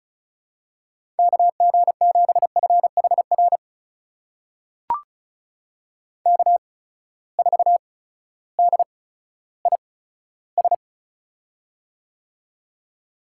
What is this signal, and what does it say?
Morse code, heard as a keyed tone, about 36 wpm, spaced as for 16 wpm, 700 Hz. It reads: KG7FHR K4DIS
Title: USA Call Signs - 8x Character Spacing - 35wpm